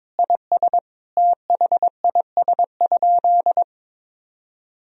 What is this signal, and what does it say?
Morse code, heard as a keyed tone, about 22 wpm, 700 Hz. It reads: IS THIS?
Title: Sentences from Top 1000 Words - Review Part 1 - 22wpm